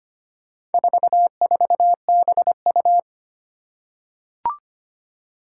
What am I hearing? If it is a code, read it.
Morse code, heard as a keyed tone, about 25 wpm, 700 Hz. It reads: 44BU
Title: Four Challenging Letter-Number Combinations 25wpm